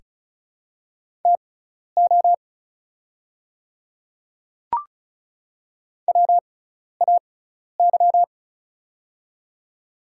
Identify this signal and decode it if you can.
Morse code, heard as a keyed tone, about 34 wpm, spaced as for 12 wpm, 700 Hz. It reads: TO WAY
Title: 6x Character Spacing - Top 100 Words - 35wpm